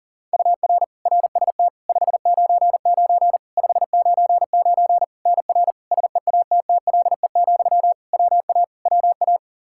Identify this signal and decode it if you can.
Morse code, heard as a keyed tone, about 40 wpm, 700 Hz. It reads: UR RST 599 599 NR SEATTLE, WA WA
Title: Practice QSO - Exchange 1 - Beginning - 40wpm